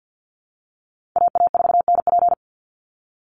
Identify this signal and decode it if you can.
Morse code, heard as a keyed tone, about 50 wpm, 700 Hz. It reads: AA4NP
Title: Call Signs Repeated 2x - 50wpm